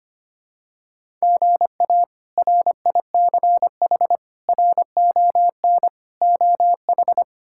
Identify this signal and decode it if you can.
Morse code, heard as a keyed tone, about 25 wpm, 700 Hz. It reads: GA RICH RON OH
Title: K1USN SST QSO Practice - 25wpm